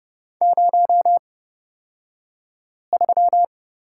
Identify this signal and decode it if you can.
Morse code, heard as a keyed tone, about 30 wpm, spaced as for 5 wpm, 700 Hz. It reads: 03
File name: Lesson 37d - Letter x - Two Characters 30wpm